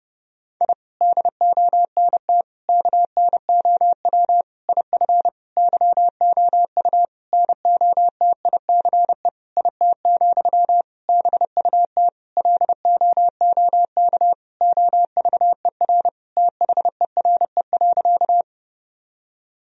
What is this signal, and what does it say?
Morse code, heard as a keyed tone, about 29 wpm, 700 Hz. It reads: I DONT KNOW IF YOU NOTICE IT, BUT LOOK OVER THERE.
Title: Sentences from Top 1000 Words - Review Part 2 - 30wpm